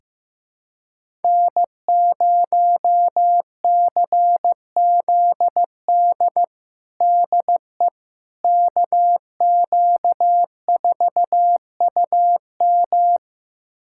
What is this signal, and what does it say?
Morse code, heard as a keyed tone, about 15 wpm, 700 Hz. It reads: N0CZD DE KQ4UM
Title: USA Callsign DE Callsign - 5 Characters - 15wpm